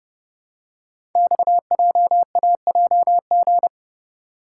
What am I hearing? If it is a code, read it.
Morse code, heard as a keyed tone, about 30 wpm, 700 Hz. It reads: XJAJG
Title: Code Groups - 5 Letters - 30wpm